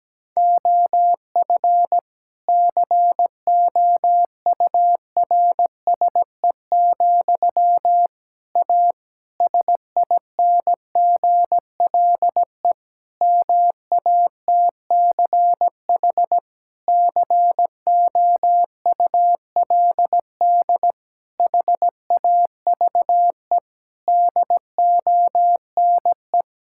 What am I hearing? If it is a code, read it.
Morse code, heard as a keyed tone, about 17 wpm, 700 Hz. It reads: OF COURSE, A SINGLE MATCH COULD HAVE DONE
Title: Sentences from Top 1600 Words 17wpm